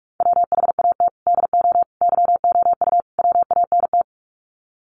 Morse code, heard as a keyed tone, about 44 wpm, 700 Hz. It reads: WHAT DO YOU WANT
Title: Sets of 4 Words 45wpm